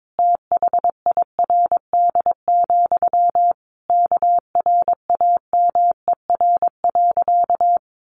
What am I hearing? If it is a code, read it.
Morse code, heard as a keyed tone, about 22 wpm, 700 Hz. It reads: THIRD, KRAMER.
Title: A Sound Of Thunder 22wpm